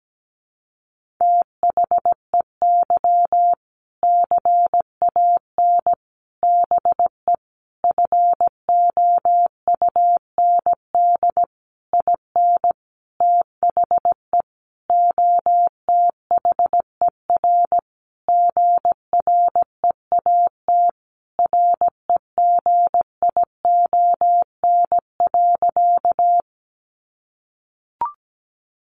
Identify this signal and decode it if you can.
Morse code, heard as a keyed tone, about 17 wpm, 700 Hz. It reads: THEY CAN BE FOUND IN THE OTHER GREAT REGION.
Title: Sentences from Top 1500 Words 17wpm